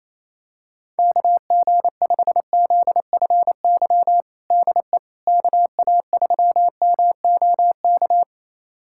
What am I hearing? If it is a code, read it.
Morse code, heard as a keyed tone, about 28 wpm, 700 Hz. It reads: KG5ZFY DE KA3MOK
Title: USA Callsign DE Callsign - 6 Characters - 28wpm